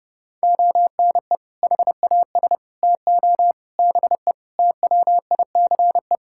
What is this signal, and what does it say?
Morse code, heard as a keyed tone, about 30 wpm, 700 Hz. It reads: ONE HAS TO BE TWICE
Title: Sentences from Top 1600 Words 30wpm